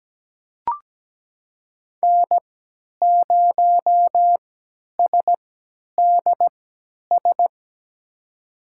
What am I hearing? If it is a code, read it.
Morse code, heard as a keyed tone, about 17 wpm, spaced as for 10 wpm, 700 Hz. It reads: N0SDS
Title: USA Call Signs - 3x Character Spacing - 17wpm